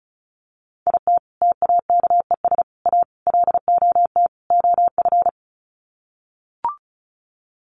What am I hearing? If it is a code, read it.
Morse code, heard as a keyed tone, about 34 wpm, 700 Hz. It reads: IT TAKES A LOT OF
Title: Sets of 5 Words Encore 35wpm